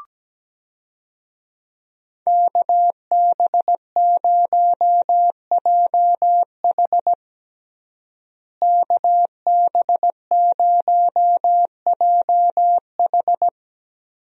Morse code, heard as a keyed tone, about 17 wpm, 700 Hz. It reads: KB0JH KB0JH
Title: USA Call Signs - 5 Characters - Repeated 3x - 17wpm